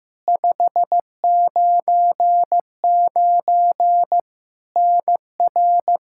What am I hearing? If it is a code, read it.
Morse code, heard as a keyed tone, about 15 wpm, 700 Hz. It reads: 599 NR
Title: Practice QSO - Exchange 1 - Beginning - 15wpm